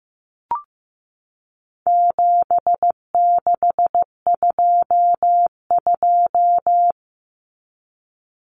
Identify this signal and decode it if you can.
Morse code, heard as a keyed tone, about 15 wpm, 700 Hz. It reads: 7622
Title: Common Cut Numbers - 4-Digit 15wpm